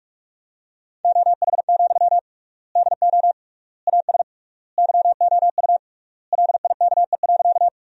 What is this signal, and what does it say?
Morse code, heard as a keyed tone, about 42 wpm, 700 Hz. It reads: OH, DO AS YOU LIKE.
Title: 3x Word-Spacing - Sentences from Top 500 Words - 45wpm